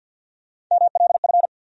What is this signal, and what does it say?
Morse code, heard as a keyed tone, about 53 wpm, 700 Hz. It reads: MLF
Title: USA Call Sign Postfixes - 3 Letters 50wpm